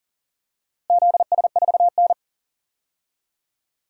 Morse code, heard as a keyed tone, about 40 wpm, 700 Hz. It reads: ZS4N
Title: DXCC-100 Call Signs - 4 Characters - Repeated 3x - 40wpm